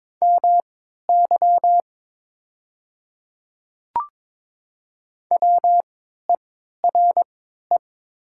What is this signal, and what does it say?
Morse code, heard as a keyed tone, about 22 wpm, spaced as for 12 wpm, 700 Hz. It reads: MY WERE
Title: 3x Character Spacing - Top 100 Words - 22wpm